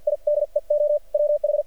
Morse code, heard as a keyed tone, about 50 wpm, spaced as for 41 wpm, 590 Hz. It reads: IDEO WH
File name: archived / CRSnet